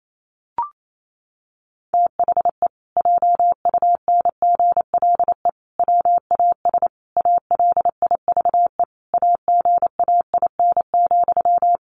Morse code, heard as a keyed tone, about 28 wpm, 700 Hz. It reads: THE JUNGLE WAS ALIVE AGAIN,
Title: A Sound Of Thunder 28wpm